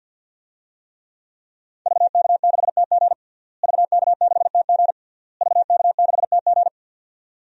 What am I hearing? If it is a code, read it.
Morse code, heard as a keyed tone, about 53 wpm, 700 Hz. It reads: VK6TG VK6TG VK6TG
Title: DXCC-100 Call Signs - 5 Characters - Repeated 3x - 50wpm